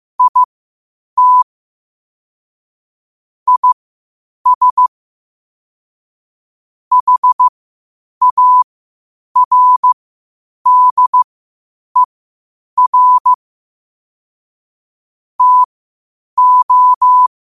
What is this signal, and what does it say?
Morse code, heard as a keyed tone, about 15 wpm, spaced as for 8 wpm, 1000 Hz. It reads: IT IS HARDER TO
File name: 8wpm